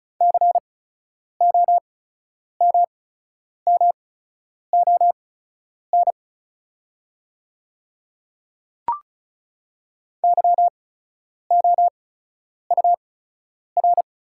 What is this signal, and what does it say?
Morse code, heard as a keyed tone, about 36 wpm, spaced as for 10 wpm, 700 Hz. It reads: COMMON YOUR
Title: 8x Character Spacing - Top 500 Words - 35wpm